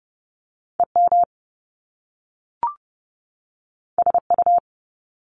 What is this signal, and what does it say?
Morse code, heard as a keyed tone, about 30 wpm, 700 Hz. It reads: EM SU
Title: Top Two Letter Combinations in Words 30wpm